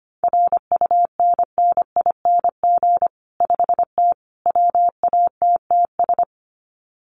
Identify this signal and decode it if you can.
Morse code, heard as a keyed tone, about 25 wpm, 700 Hz. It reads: RUNNING 5T WATTS
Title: Common QSO Phrases 25wpm